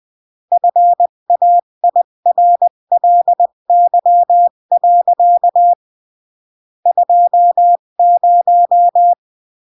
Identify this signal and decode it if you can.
Morse code, heard as a keyed tone, about 20 wpm, 700 Hz. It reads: FAIRLY. 20
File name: CW / jodorowsky / 82 commandments 20-15